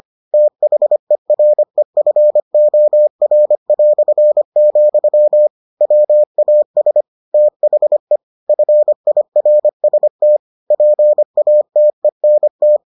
Morse code, heard as a keyed tone, about 25 wpm, 600 Hz. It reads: THEREFOR", WAS THE FIRST PATENT